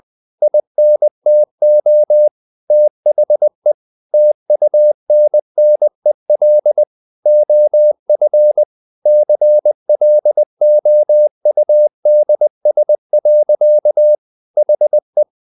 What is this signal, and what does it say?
Morse code, heard as a keyed tone, about 20 wpm, 600 Hz. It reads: INTO THE TUNNEL OF CLOUDS. HE
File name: the old man and the sea part 2 20wpm